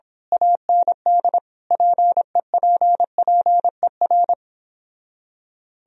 Morse code, heard as a keyed tone, about 26 wpm, 700 Hz. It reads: AND PEPPER
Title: Speed-Racing - Sets of 5 Words 22wpm